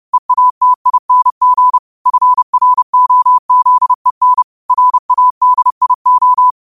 Morse code, heard as a keyed tone, about 30 wpm, 1000 Hz. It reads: EATING FROZEN RADIO